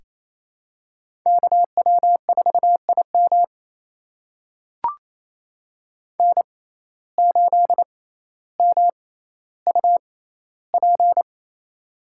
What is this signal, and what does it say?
Morse code, heard as a keyed tone, about 28 wpm, spaced as for 14 wpm, 700 Hz. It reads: KW4IM N8MUP
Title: USA Call Signs - 5 Characters - 6x Character Spacing - 28wpm